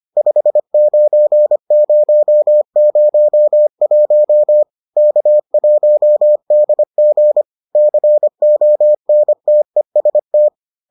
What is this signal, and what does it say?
Morse code, heard as a keyed tone, about 25 wpm, 600 Hz. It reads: 59001 K1DG CONTEST